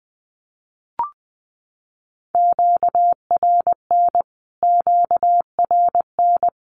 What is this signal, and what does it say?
Morse code, heard as a keyed tone, about 19 wpm, 700 Hz. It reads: QRN QRN
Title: Q-Codes Repeated 3x 20wpm